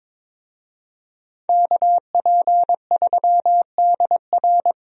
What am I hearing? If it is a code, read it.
Morse code, heard as a keyed tone, about 21 wpm, 700 Hz. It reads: KP3DR
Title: USA Call Signs - 5 Characters - Repeated 2x - 22wpm